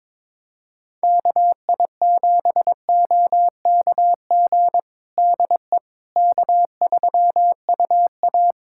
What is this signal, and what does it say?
Morse code, heard as a keyed tone, about 22 wpm, 700 Hz. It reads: KI7OKG DE K3UA
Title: Practice QSO - Exchange 4 - Ending - 22wpm